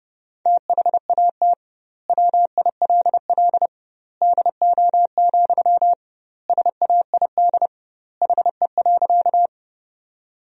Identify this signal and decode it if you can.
Morse code, heard as a keyed tone, about 30 wpm, 700 Hz. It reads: THAT WILL DO, SAID HE.
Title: 2x Word-Spacing - Sentences from Top 100 Words - 30wpm